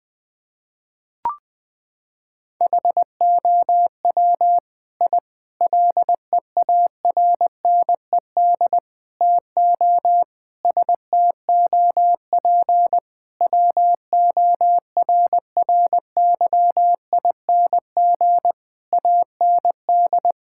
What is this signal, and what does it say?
Morse code, heard as a keyed tone, about 20 wpm, 700 Hz. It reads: HOW I LEARNED TO STOP WORRYING AND
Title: Sentences with 8-Word Phrases - 20wpm